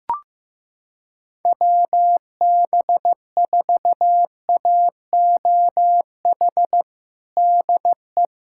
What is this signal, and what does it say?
Morse code, heard as a keyed tone, about 15 wpm, 700 Hz. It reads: WB4AOH DE
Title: USA Callsign DE Callsign 15wpm